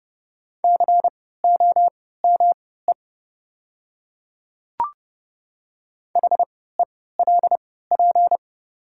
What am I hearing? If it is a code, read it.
Morse code, heard as a keyed tone, about 30 wpm, spaced as for 17 wpm, 700 Hz. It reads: COME HELP
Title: 3x Character Spacing - Top 300 Words - 30wpm